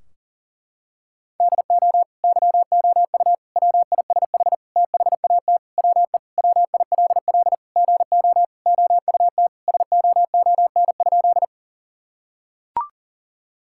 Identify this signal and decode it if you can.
Morse code, heard as a keyed tone, about 37 wpm, 700 Hz. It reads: DO YOU WISH THAT WE WILL GO OUT SOON?
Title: Sentences from Top 1600 Words 40wpm